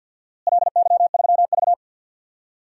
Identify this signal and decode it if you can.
Morse code, heard as a keyed tone, about 52 wpm, 700 Hz. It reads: LY3V
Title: DXCC-100 Call Signs - 4 Characters - Repeated 2x - 50wpm